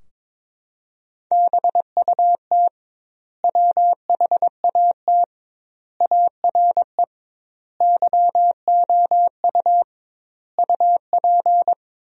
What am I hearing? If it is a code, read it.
Morse code, heard as a keyed tone, about 23 wpm, 700 Hz. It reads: BUT WHAT ARE YOU UP
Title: 2x Word-Spacing - Sentences from Top 100 Words - 22wpm